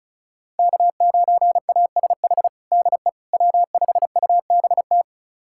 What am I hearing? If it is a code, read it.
Morse code, heard as a keyed tone, about 36 wpm, 690 Hz. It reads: K9ASH DE W5UBT
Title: USA Callsign DE Callsign - 5 Characters - 35wpm